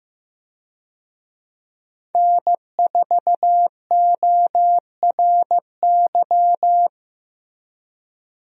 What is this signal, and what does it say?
Morse code, heard as a keyed tone, about 15 wpm, 700 Hz. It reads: N4ORY
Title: USA Call Signs - 5 Characters - Repeated 2x - 15wpm